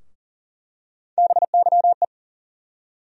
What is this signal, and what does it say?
Morse code, heard as a keyed tone, about 40 wpm, 700 Hz. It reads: BYE
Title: USA Call Sign Postfixes - 3 Letters 40wpm